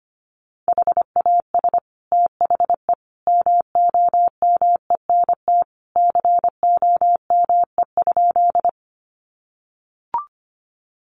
Morse code, heard as a keyed tone, about 25 wpm, 700 Hz. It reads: HAS THE MOMENT COME?
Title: Sentences from Top 1000 Words 25wpm